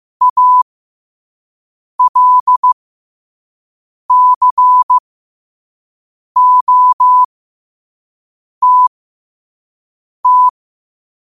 Morse code, heard as a keyed tone, about 15 wpm, spaced as for 5 wpm, 1000 Hz. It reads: ALCOTT
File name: qotdfiles / 5wpm